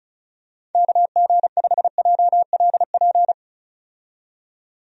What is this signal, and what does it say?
Morse code, heard as a keyed tone, about 36 wpm, 700 Hz. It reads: KG5JLP
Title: USA Call Signs - 6 Characters - 35wpm